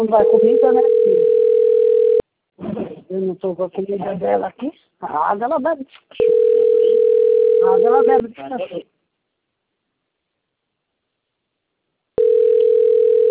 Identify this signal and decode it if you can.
Ringback tone (North America), 440 + 480 Hz, with speech.
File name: rg-1600-unknown- - - .46121